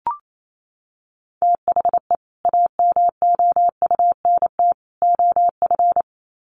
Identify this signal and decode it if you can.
Morse code, heard as a keyed tone, about 28 wpm, 700 Hz. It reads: THE AMOUNT OF
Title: Sets of 3 Words 28wpm